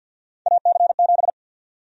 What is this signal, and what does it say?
Morse code, heard as a keyed tone, about 48 wpm, 700 Hz. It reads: AC7
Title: USA Call Sign Prefixes - 50wpm